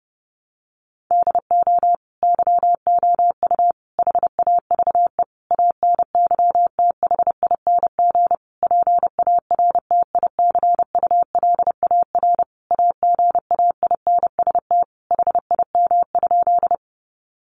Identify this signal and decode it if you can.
Morse code, heard as a keyed tone, about 30 wpm, 700 Hz. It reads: DO YOU HAVE ANYTHING PARTICULAR AGAINST HIM?
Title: Sentences from Top 1000 Words 30wpm